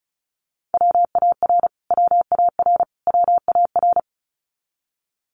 Morse code, heard as a keyed tone, about 34 wpm, 700 Hz. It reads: WAR WAR WAR
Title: Top 500 Words Repeated 35wpm